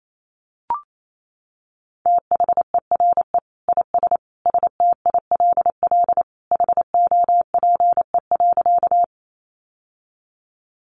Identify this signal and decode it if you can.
Morse code, heard as a keyed tone, about 28 wpm, 700 Hz. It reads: THERE IS STILL HOPE.
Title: Sentences from Top 1000 Words 28wpm